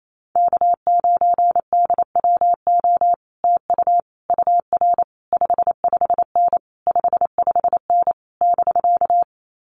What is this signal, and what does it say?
Morse code, heard as a keyed tone, about 28 wpm, 700 Hz. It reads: K9DWO TU UR 55N 55N <BK>
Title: Common QSO Phrases Encore 28wpm